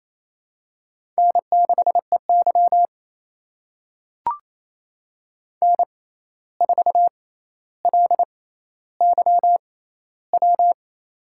Morse code, heard as a keyed tone, about 28 wpm, spaced as for 13 wpm, 700 Hz. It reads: N6EY N4LYW
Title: USA Call Signs - 6x Character Spacing - 28wpm